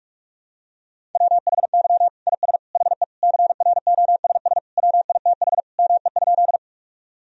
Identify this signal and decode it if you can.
Morse code, heard as a keyed tone, about 46 wpm, 700 Hz. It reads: WHY IS HE CROSS WITH ME?
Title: Sentences from Top 1000 Words - Review Part 1 - 45wpm